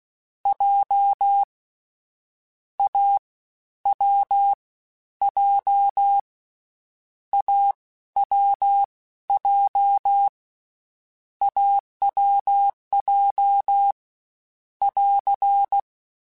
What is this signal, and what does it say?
Morse code, heard as a keyed tone, about 16 wpm, spaced as for 13 wpm, 800 Hz. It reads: J A W J AWJ AWJ <AR>
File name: Difficult Combos / Difficult Combos-mp3s / AWJ ShortRun @15